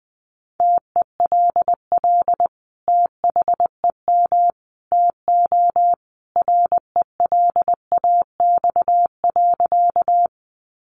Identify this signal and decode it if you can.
Morse code, heard as a keyed tone, about 19 wpm, 700 Hz. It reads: TELL THEM TO RELAX.
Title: Sentences from Top 1400 Words 20wpm